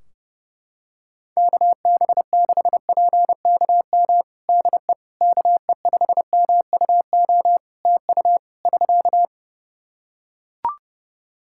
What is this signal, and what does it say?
Morse code, heard as a keyed tone, about 32 wpm, 700 Hz. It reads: KB6PKM DE KE5MUO TU <SK>
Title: Practice QSO - Exchange 4 - Ending - 30wpm